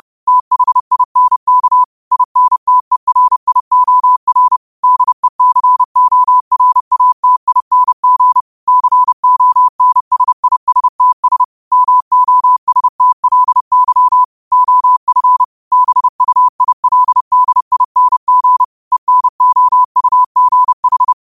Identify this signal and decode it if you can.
Morse code, heard as a keyed tone, about 30 wpm, 1000 Hz. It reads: THINK INTERIOR DECORATING CONSISTS MOSTLY OF BUILDING ENOUGH